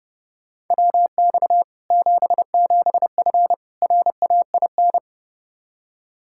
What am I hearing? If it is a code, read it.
Morse code, heard as a keyed tone, about 30 wpm, 700 Hz. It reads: WX 77F RAIN
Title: Practice QSO - Exchange 2 - Info - 30wpm